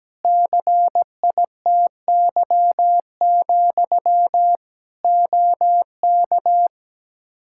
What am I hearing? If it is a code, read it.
Morse code, heard as a keyed tone, about 17 wpm, 700 Hz. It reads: CITY, OK
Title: Practice QSO - Exchange 1 - Beginning - 17wpm